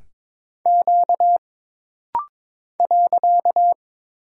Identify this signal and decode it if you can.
Morse code, heard as a keyed tone, about 22 wpm, 700 Hz. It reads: Q .
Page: This rapid-fire timing is geared to push your ICR (Instant Character Recognition) skills.